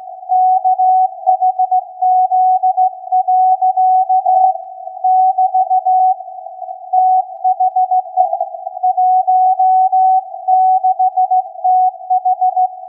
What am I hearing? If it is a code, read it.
Morse code, heard as a keyed tone, about 16 wpm, 740 Hz. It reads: KHZ. <BT> THE 16TH
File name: iz7slz / VLF / SAQ 16-11-2022